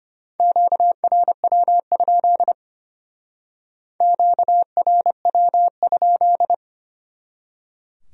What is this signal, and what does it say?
Morse code, heard as a keyed tone, about 31 wpm, 700 Hz. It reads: QRW? QRW?
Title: Speed-Racing Q-Codes - 25wpm